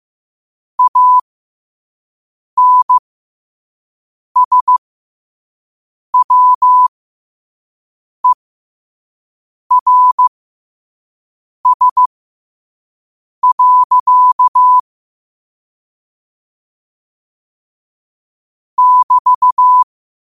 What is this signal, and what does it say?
Morse code, heard as a keyed tone, about 15 wpm, spaced as for 5 wpm, 1000 Hz. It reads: ANSWERS. <BT>